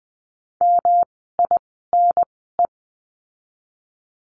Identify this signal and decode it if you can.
Morse code, heard as a keyed tone, about 20 wpm, spaced as for 15 wpm, 700 Hz. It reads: MINE
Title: 108 Lesson 18f - Letter m - Words 20wpm